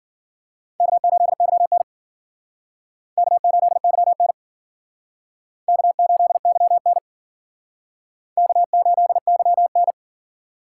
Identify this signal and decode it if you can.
Morse code, heard as a keyed tone, about 53 wpm, 700 Hz. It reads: K8YN K8YN K8YN K8YN